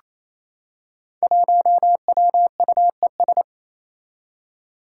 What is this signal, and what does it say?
Morse code, heard as a keyed tone, about 28 wpm, 700 Hz. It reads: 1WUES
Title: Five Challenging Letter-Number Combinations 28wpm